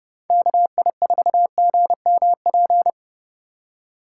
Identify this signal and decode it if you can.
Morse code, heard as a keyed tone, about 35 wpm, 690 Hz. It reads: KI4GMP
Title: Speed Racing - USA Call Signs 30wpm